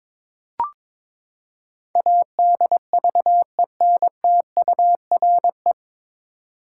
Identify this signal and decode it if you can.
Morse code, heard as a keyed tone, about 22 wpm, 700 Hz. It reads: ADVENTURE
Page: Morse Code Ninja practice 22wpm